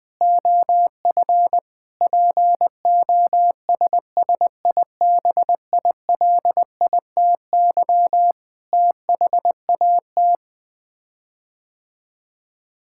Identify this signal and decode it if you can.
Morse code, heard as a keyed tone, about 20 wpm, 700 Hz. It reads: OF POSSIBILITY THAT
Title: Sets of 9 Words - 20wpm